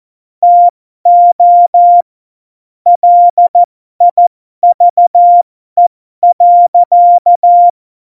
Morse code, heard as a keyed tone, about 14 wpm, spaced as for 12 wpm, 700 Hz. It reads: TO LIVE.
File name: Morsekurs / Witze